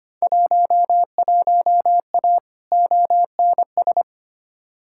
Morse code, heard as a keyed tone, about 25 wpm, 700 Hz. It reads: 11A ONS
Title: Field Day Exchanges Encore - 25wpm